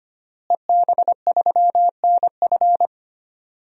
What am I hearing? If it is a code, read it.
Morse code, heard as a keyed tone, about 25 wpm, 700 Hz. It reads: EB3NF
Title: DXCC-100 Call Signs - 5 Characters - Repeated 3x - 25wpm